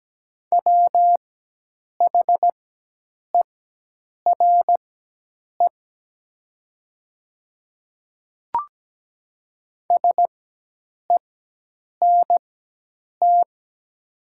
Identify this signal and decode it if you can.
Morse code, heard as a keyed tone, about 17 wpm, spaced as for 8 wpm, 700 Hz. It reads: WHERE SENT
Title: 4x Character Spacing - Top 200 Words - 17wpm